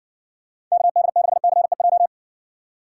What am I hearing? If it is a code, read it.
Morse code, heard as a keyed tone, about 55 wpm, 700 Hz. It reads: KD6CEJ